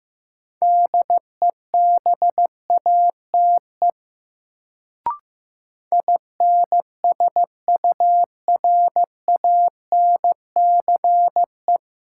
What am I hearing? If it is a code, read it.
Morse code, heard as a keyed tone, about 15 wpm, 700 Hz. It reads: DEBATE INSURANCE
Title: Sentences from Top 1100 Words - New Word Drill - 15wpm